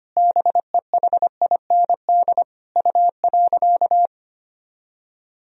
Morse code, heard as a keyed tone, about 25 wpm, 700 Hz. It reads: BEHIND U.
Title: Sentences from Top 500 Words and Abbreviations 25wpm